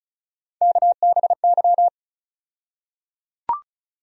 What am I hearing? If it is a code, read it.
Morse code, heard as a keyed tone, about 34 wpm, 700 Hz. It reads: KBY
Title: USA Call Sign Postfixes - 3 Letters 35wpm